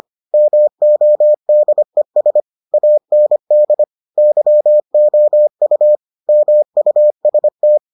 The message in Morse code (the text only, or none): MODES AND YOU MUST